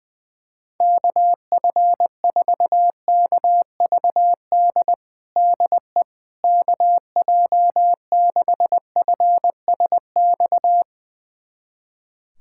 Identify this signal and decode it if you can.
Morse code, heard as a keyed tone, about 21 wpm, 700 Hz. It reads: KF4KVD DE KJ6FSX
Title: USA Callsign DE Callsign - 6 Characters - 20wpm